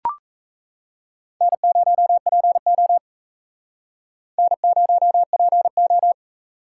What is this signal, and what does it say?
Morse code, heard as a keyed tone, about 39 wpm, 700 Hz. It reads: N0PO N0PO